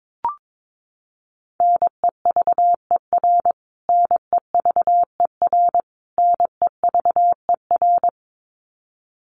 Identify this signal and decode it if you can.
Morse code, heard as a keyed tone, about 22 wpm, 700 Hz. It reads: NEVER NEVER NEVER
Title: Top 300 Words Repeated 22wpm